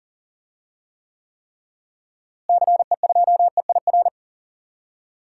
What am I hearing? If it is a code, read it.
Morse code, heard as a keyed tone, about 40 wpm, 700 Hz. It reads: CE2EIR
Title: DXCC-100 Call Signs - 6 Characters - Repeated 3x - 40wpm